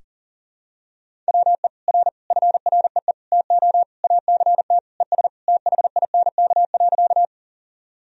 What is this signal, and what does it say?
Morse code, heard as a keyed tone, about 40 wpm, 700 Hz. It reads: WE R FREE TO ACT ES THINK.
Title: Sentences from Top 500 Words and Abbreviations 40wpm